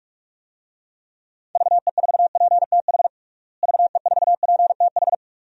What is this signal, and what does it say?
Morse code, heard as a keyed tone, about 46 wpm, 700 Hz. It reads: VE4PTH VE4PTH
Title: DXCC-100 Call Signs - 6 Characters - Repeated 2x - 45wpm